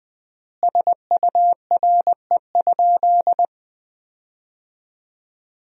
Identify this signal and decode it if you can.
Morse code, heard as a keyed tone, about 20 wpm, 700 Hz. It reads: SURE?
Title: 3x Word-Spacing - Sentences from Top 300 Words - 20wpm